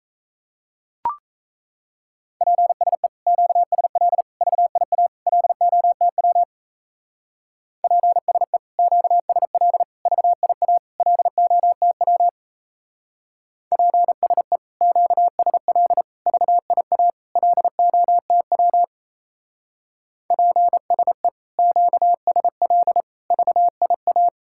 Morse code, heard as a keyed tone, about 34 wpm, 700 Hz. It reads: PSE QSL VIA LOTW PSE QSL VIA LOTW PSE QSL VIA LOTW PSE QSL VIA
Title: Speed-Racing Common QSO Phrases Encore - 28wpm